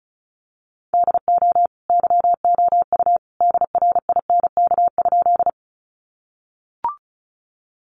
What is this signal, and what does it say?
Morse code, heard as a keyed tone, about 34 wpm, 700 Hz. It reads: DO YOU DRINK?
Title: Sentences from Top 1500 Words 35wpm